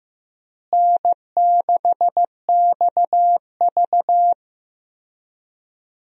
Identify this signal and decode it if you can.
Morse code, heard as a keyed tone, about 15 wpm, 700 Hz. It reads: N6XV
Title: USA Call Signs - 4 Characters - Repeated 3x - 15wpm